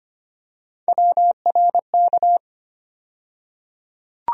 Morse code, heard as a keyed tone, about 25 wpm, 700 Hz. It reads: WRK
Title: USA Call Sign Postfixes - 3 Letters 25wpm